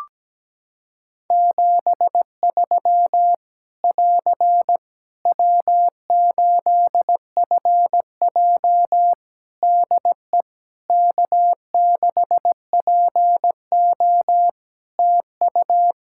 Morse code, heard as a keyed tone, about 17 wpm, 700 Hz. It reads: 73 <AR> W8FJ DE K6PO TU
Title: Practice QSO - Exchange 4 - Ending - 17wpm